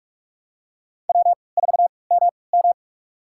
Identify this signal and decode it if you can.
Morse code, heard as a keyed tone, about 46 wpm, spaced as for 26 wpm, 700 Hz. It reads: W4MM
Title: USA Call Signs - 4 Characters - 3x Character Spacing - 45wpm